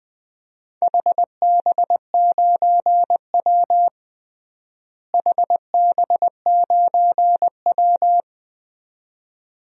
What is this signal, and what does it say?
Morse code, heard as a keyed tone, about 21 wpm, 700 Hz. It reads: HB9W HB9W
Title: DXCC-100 Call Signs - 4 Characters - Repeated 2x - 20wpm